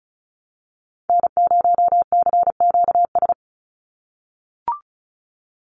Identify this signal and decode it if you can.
Morse code, heard as a keyed tone, about 34 wpm, 700 Hz. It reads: N0CQS
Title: USA Call Signs 35wpm